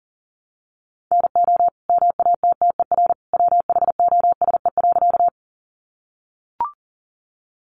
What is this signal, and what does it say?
Morse code, heard as a keyed tone, about 40 wpm, 700 Hz. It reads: NO MATTER WHOSE.
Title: Sentences from Top 1200 Words 40wpm